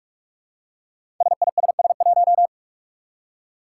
Morse code, heard as a keyed tone, about 46 wpm, 700 Hz. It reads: SISS1
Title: Five Challenging Letter-Number Combinations 45wpm